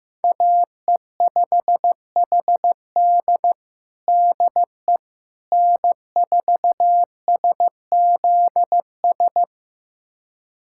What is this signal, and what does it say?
Morse code, heard as a keyed tone, about 16 wpm, 700 Hz. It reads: AE5HD DE N4SZS
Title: USA Callsign DE Callsign - 5 Characters - 15wpm